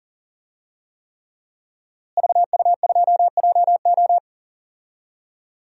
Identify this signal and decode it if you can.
Morse code, heard as a keyed tone, about 40 wpm, 700 Hz. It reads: VU2JO
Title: DXCC-100 Call Signs - 5 Characters - Repeated 3x - 40wpm